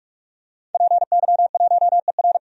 Morse code, heard as a keyed tone, about 46 wpm, 700 Hz. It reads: PY1ER
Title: DXCC-100 Call Signs - 5 Characters - 45wpm